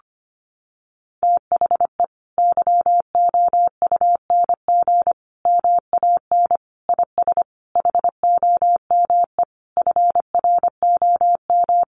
Morse code, heard as a keyed tone, about 25 wpm, 700 Hz. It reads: THE YOUNG MAN IS HOME FROM
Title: Sentences from Top 1300 Words 25wpm